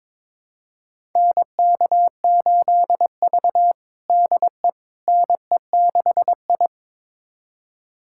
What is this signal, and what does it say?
Morse code, heard as a keyed tone, about 22 wpm, 700 Hz. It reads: NK8V DE NE6I
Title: USA Callsign DE Callsign - 4 Characters - 22wpm